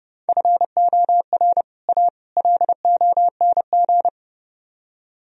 Morse code, heard as a keyed tone, about 30 wpm, 700 Hz. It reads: FOR A LONG